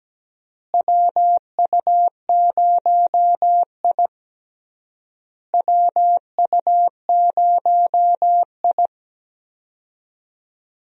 Morse code, heard as a keyed tone, about 17 wpm, 700 Hz. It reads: WU0I WU0I
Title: USA Call Signs - 4 Characters - Repeated 3x - 17wpm